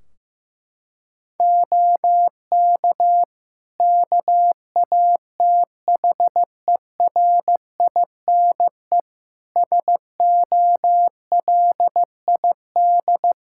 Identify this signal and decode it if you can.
Morse code, heard as a keyed tone, about 15 wpm, 700 Hz. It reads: OK KATHERINE SOLID
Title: Practice QSO - Exchange 3 - More Info - 15wpm